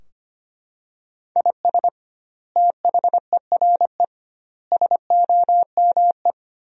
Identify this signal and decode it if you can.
Morse code, heard as a keyed tone, about 25 wpm, 700 Hz. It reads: IS THERE SOME
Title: 2x Word-Spacing - Sentences from Top 200 Words - 25wpm